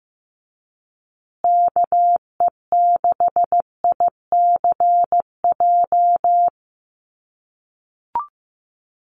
Morse code, heard as a keyed tone, about 15 wpm, 700 Hz. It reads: KE6ICJ
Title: USA Call Signs Repeated 2x 15wpm